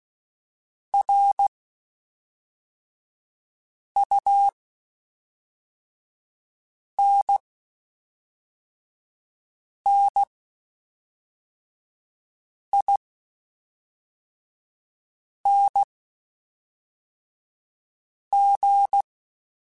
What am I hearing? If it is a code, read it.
Morse code, heard as a keyed tone, about 16 wpm, spaced as for 3 wpm, 800 Hz. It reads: RUNNING
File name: (spaced) / QSOs Spaced-mp3s / kc7haa @5 sp